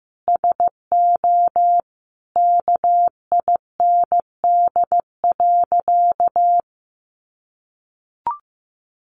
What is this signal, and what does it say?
Morse code, heard as a keyed tone, about 15 wpm, 700 Hz. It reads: SO KIND.
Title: Sentences from Top 500 Words 15wpm